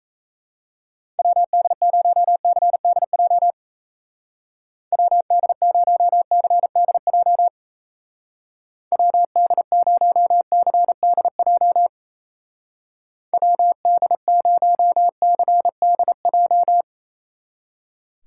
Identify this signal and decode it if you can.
Morse code, heard as a keyed tone, about 34 wpm, 700 Hz. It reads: WD0CDJ WD0CDJ WD0CDJ WD0CDJ
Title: Speed Racing - USA Call Signs 28wpm